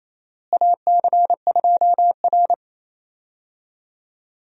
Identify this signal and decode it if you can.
Morse code, heard as a keyed tone, about 28 wpm, 700 Hz. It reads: AC2R
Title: USA Call Signs - 4 Characters - Repeated 2x - 28wpm